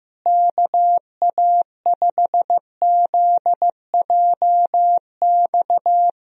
Morse code, heard as a keyed tone, about 15 wpm, 700 Hz. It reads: KA5ZJX
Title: USA Call Signs - 6 Characters - Repeated 3x - 15wpm